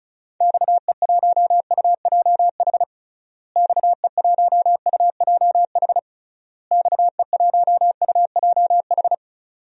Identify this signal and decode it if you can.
Morse code, heard as a keyed tone, about 34 wpm, 700 Hz. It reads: XE1UJH XE1UJH XE1UJH
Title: DXCC-100 Call Signs - 6 Characters - Repeated 3x - 35wpm